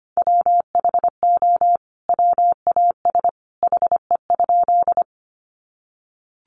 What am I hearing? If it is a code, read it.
Morse code, heard as a keyed tone, about 25 wpm, 700 Hz. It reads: WHO WAS HE?
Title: Sentences from Top 100 Words 25wpm